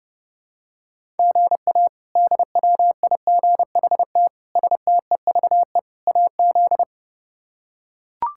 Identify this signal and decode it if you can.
Morse code, heard as a keyed tone, about 30 wpm, 700 Hz. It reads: GA DWIGHT STEVE AZ
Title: K1USN SST QSO Practice - 30wpm